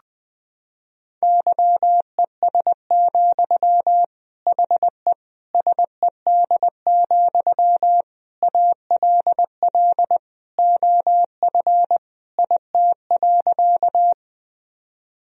Morse code, heard as a keyed tone, about 21 wpm, 700 Hz. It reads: YES, HE SED, ALL OF IT.
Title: Sentences from Top 500 Words and Abbreviations 20wpm